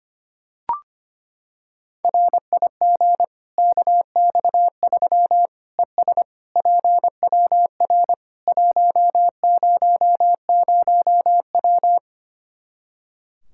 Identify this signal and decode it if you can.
Morse code, heard as a keyed tone, about 25 wpm, 700 Hz. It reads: RIG KX3 ES PWR 100W
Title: Practice QSO - Exchange 2 - Info - 25wpm